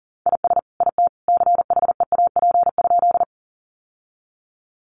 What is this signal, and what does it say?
Morse code, heard as a keyed tone, about 40 wpm, 700 Hz. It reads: IS IT CHEAP?
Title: Sentences from Top 1200 Words 40wpm